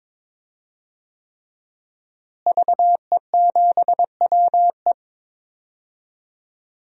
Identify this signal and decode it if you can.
Morse code, heard as a keyed tone, about 21 wpm, 700 Hz. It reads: VE7WE
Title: DXCC-100 Call Signs - 5 Characters - Repeated 3x - 22wpm